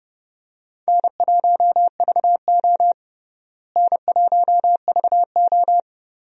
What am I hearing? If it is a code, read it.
Morse code, heard as a keyed tone, about 30 wpm, 700 Hz. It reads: N1VO N1VO
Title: USA Call Signs - 4 Characters - Repeated 2x - 30wpm